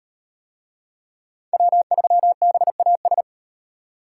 Morse code, heard as a keyed tone, about 35 wpm, 700 Hz. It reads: W3BAS